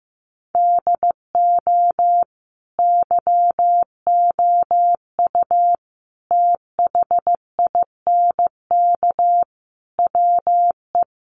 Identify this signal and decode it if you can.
Morse code, heard as a keyed tone, about 15 wpm, 700 Hz. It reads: DO YOU THINK WE
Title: Sentences from Top 1200 Words 15wpm